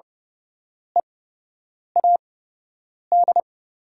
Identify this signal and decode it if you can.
Morse code, heard as a keyed tone, about 30 wpm, spaced as for 8 wpm, 700 Hz. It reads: EAD
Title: 8x Character Spacing - Top 300 Words - 30wpm